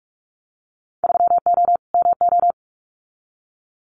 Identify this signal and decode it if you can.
Morse code, heard as a keyed tone, about 46 wpm, 700 Hz. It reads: 3O MO
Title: Winter Field Day Exchanges Encore 45wpm